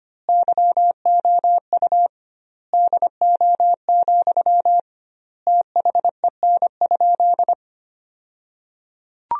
Morse code, heard as a keyed tone, about 25 wpm, 700 Hz. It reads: YOU DO, THEN?
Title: 2x Word-Spacing - Sentences from Top 100 Words - 25wpm